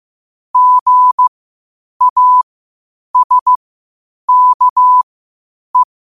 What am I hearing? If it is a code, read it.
Morse code, heard as a keyed tone, about 15 wpm, spaced as for 8 wpm, 1000 Hz. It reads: GASKE
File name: qotdfiles / 8wpm